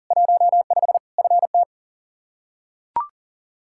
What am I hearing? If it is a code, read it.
Morse code, heard as a keyed tone, about 40 wpm, 700 Hz. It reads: 15 FT
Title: K1USN SST QSO Practice 40wpm